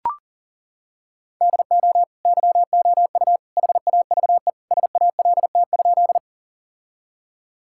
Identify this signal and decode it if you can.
Morse code, heard as a keyed tone, about 40 wpm, 700 Hz. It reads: DO YOU HAVE SALT?
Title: Sentences from Top 1000 Words - Review Part 5 - 40wpm